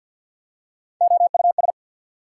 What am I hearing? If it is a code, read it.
Morse code, heard as a keyed tone, about 53 wpm, 690 Hz. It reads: OUS
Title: Prefixes and Suffixes 50wpm